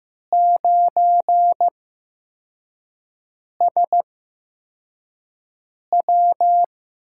Morse code, heard as a keyed tone, about 15 wpm, spaced as for 4 wpm, 700 Hz. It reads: 9SW
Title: USA Call Signs - 5 Characters - 8x Character Spacing - 15wpm